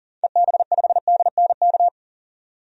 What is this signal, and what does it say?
Morse code, heard as a keyed tone, about 38 wpm, 700 Hz. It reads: EB5DNK
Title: DXCC-100 Call Signs - 6 Characters - 40wpm